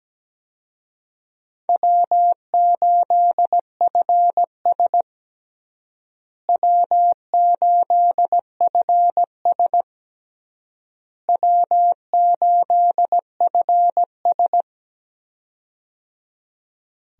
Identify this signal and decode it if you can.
Morse code, heard as a keyed tone, about 17 wpm, 700 Hz. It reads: W8FS W8FS W8FS
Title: USA Call Signs - 4 Characters - Repeated 3x - 17wpm